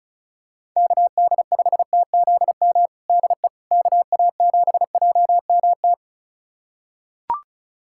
Morse code, heard as a keyed tone, about 36 wpm, 700 Hz. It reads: KD5TZM DE KA7JMT
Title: USA Callsign DE Callsign 35wpm